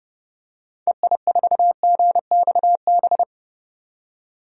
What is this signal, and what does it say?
Morse code, heard as a keyed tone, about 30 wpm, 700 Hz. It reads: EI4GXB
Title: DXCC-100 Call Signs - 6 Characters - 30wpm